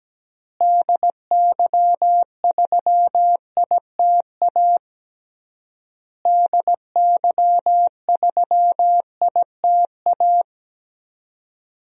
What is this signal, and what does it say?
Morse code, heard as a keyed tone, about 17 wpm, 700 Hz. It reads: DY3ITA DY3ITA
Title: DXCC-100 Call Signs - 6 Characters - Repeated 3x - 17wpm